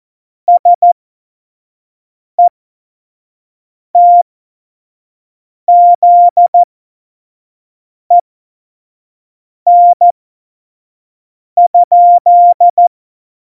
Morse code, heard as a keyed tone, about 14 wpm, spaced as for 5 wpm, 700 Hz. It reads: SETZEN?
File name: Morsekurs / Witze / Witz 10. Oktober 17, Tempo 25